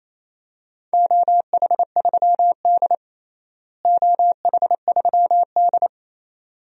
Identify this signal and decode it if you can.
Morse code, heard as a keyed tone, about 28 wpm, 700 Hz. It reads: OH3D OH3D
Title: DXCC-100 Call Signs - 4 Characters - Repeated 2x - 28wpm